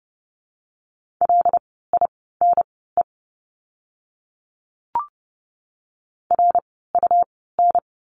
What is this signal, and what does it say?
Morse code, heard as a keyed tone, about 30 wpm, spaced as for 17 wpm, 700 Hz. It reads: LINE RUN
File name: course / 30:10wpm / 115 Lesson 19f - Letter w - Words 30wpm